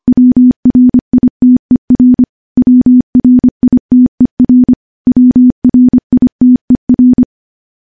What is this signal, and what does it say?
Morse code, heard as a keyed tone, about 25 wpm, 250 Hz. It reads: WRITER WRITER WRITER
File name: morse / morse-training-25WPM